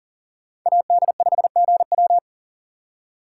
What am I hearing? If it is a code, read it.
Morse code, heard as a keyed tone, about 52 wpm, 700 Hz. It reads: AD5GW